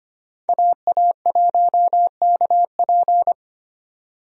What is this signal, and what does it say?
Morse code, heard as a keyed tone, about 25 wpm, 700 Hz. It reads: AA1KP
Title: USA Call Signs - 5 Characters - 25wpm